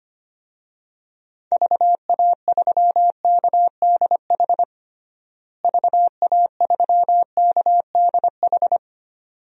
Morse code, heard as a keyed tone, about 25 wpm, 700 Hz. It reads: VA3KDH VA3KDH
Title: DXCC-100 Call Signs - 6 Characters - Repeated 2x - 25wpm